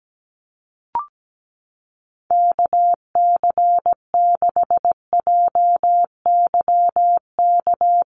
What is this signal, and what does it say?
Morse code, heard as a keyed tone, about 17 wpm, 700 Hz. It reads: KC6JYK
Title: USA Call Signs Repeated 3x 17wpm